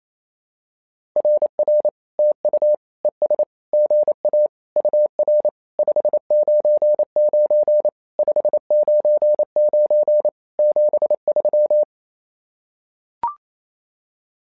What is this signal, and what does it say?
Morse code, heard as a keyed tone, about 27 wpm, 600 Hz. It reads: RR TU ES GA UR 599 599 73
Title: SOTA QSO Practice 28wpm